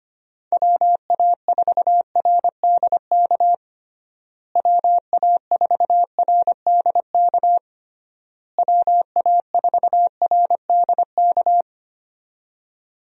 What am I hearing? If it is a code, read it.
Morse code, heard as a keyed tone, about 25 wpm, 700 Hz. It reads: WA4RDK WA4RDK WA4RDK
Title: USA Call Signs - 6 Characters - Repeated 3x - 25wpm